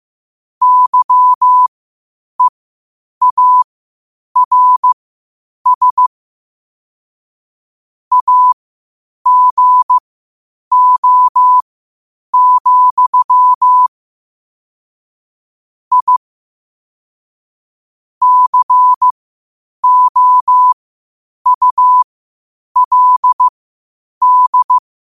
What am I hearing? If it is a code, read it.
Morse code, heard as a keyed tone, about 15 wpm, spaced as for 8 wpm, 1000 Hz. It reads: YEARS AGO, I COULD
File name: qotdfiles / 8wpm